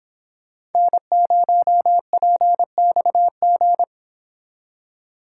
Morse code, heard as a keyed tone, about 29 wpm, 700 Hz. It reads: N0PXG